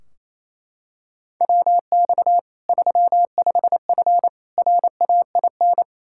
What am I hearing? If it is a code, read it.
Morse code, heard as a keyed tone, about 28 wpm, 700 Hz. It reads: WX 35F RAIN
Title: K1USN SST QSO Practice 28wpm